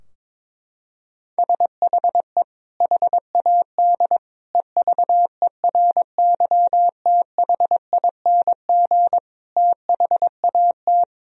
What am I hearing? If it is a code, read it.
Morse code, heard as a keyed tone, about 22 wpm, 700 Hz. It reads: SHE HAD EVERYTHING THAT
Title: Sentences from Top 1000 Words - Review Part 4 - 22wpm